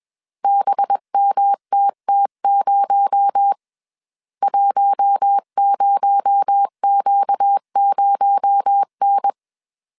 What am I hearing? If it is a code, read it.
Morse code, heard as a keyed tone, about 21 wpm, 800 Hz. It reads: BMTT0 10Q0N
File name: CW / DET / 1 6 1500C S A 20wpm 2048